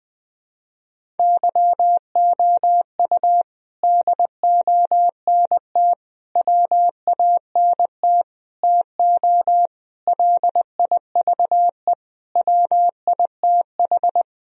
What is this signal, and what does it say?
Morse code, heard as a keyed tone, about 20 wpm, 700 Hz. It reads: YOU DONT WANT TO LIVE WITH
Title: Sentences from Top 1000 Words - Review Part 1 - 20wpm